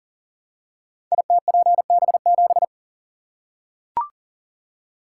Morse code, heard as a keyed tone, about 40 wpm, 700 Hz. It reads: ITPB7
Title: Code Groups - 5 Letter:Numbers - 40wpm